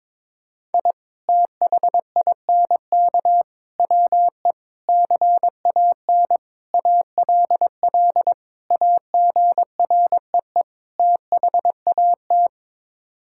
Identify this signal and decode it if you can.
Morse code, heard as a keyed tone, about 22 wpm, 700 Hz. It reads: I THINK WE CAN ALL AGREE THAT
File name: practice / 22wpm / Sets of 7 Words - 22wpm